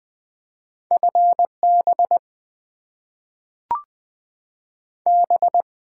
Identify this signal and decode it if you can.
Morse code, heard as a keyed tone, about 20 wpm, 700 Hz. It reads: FB B
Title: Top 90 QSO Elements - 8x Character Spacing - 20wpm